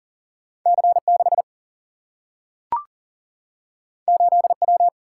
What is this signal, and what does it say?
Morse code, heard as a keyed tone, about 40 wpm, 700 Hz. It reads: C6 8W
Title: Single-Letter Single-Number 40wpm